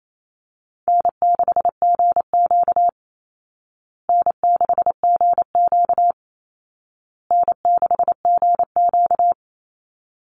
Morse code, heard as a keyed tone, about 28 wpm, 700 Hz. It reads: N6GQ N6GQ N6GQ
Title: Easy Call Signs Repeated 3x - 28wpm